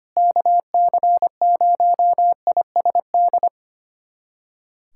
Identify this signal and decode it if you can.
Morse code, heard as a keyed tone, about 25 wpm, 700 Hz. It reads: KC0ISD
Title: USA Callsign DE Callsign - 6 Characters - 25wpm